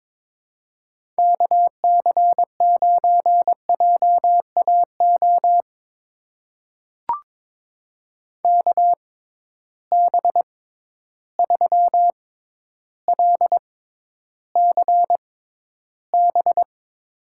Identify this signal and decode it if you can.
Morse code, heard as a keyed tone, about 22 wpm, spaced as for 11 wpm, 700 Hz. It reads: KC9JAO KB3LCB
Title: USA Call Signs - 6 Characters - 6x Character Spacing - 22wpm